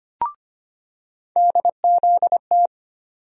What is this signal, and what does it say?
Morse code, heard as a keyed tone, about 25 wpm, 700 Hz. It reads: DZT
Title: USA Call Sign Postfixes - 3 Letters 25wpm